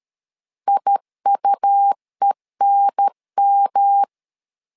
Morse code, heard as a keyed tone, about 13 wpm, 800 Hz. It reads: IUENM
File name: LEZIONI / 3 MWRO / 3 M 12wpm 1541